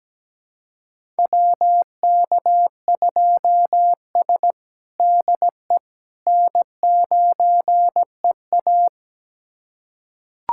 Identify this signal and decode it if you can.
Morse code, heard as a keyed tone, about 17 wpm, 700 Hz. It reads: WK2S DE N9EA
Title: USA Callsign DE Callsign - 4 Characters - 17wpm